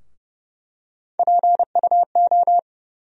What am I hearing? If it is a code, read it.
Morse code, heard as a keyed tone, about 31 wpm, 700 Hz. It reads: PUO
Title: USA Call Sign Postfixes - 3 Letters 30wpm